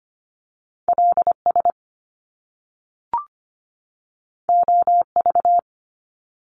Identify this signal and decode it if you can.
Morse code, heard as a keyed tone, about 25 wpm, 700 Hz. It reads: LS OV
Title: Top Two Letter Combinations in Words 25wpm